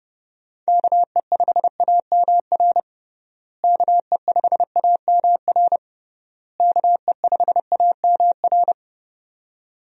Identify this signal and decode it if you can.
Morse code, heard as a keyed tone, about 32 wpm, 700 Hz. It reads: KE5AMR KE5AMR KE5AMR
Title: USA Call Signs - 6 Characters - Repeated 3x - 30wpm